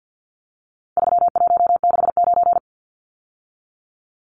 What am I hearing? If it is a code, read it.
Morse code, heard as a keyed tone, about 50 wpm, 700 Hz. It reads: 3169
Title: Numbers - 4-Digit 50wpm